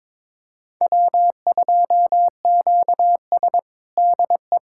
Morse code, heard as a keyed tone, about 22 wpm, 700 Hz. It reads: W2QS DE
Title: USA Callsign DE Callsign - 4 Characters - 22wpm